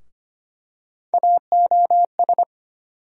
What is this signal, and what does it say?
Morse code, heard as a keyed tone, about 25 wpm, 700 Hz. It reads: AOS
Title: USA Call Sign Postfixes - 3 Letters 25wpm